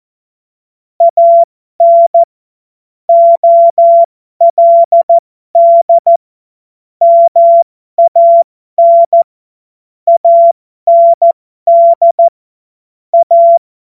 Morse code, heard as a keyed tone, about 14 wpm, spaced as for 12 wpm, 680 Hz. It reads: AN OLD MAN AND A
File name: Morsekurs / Witze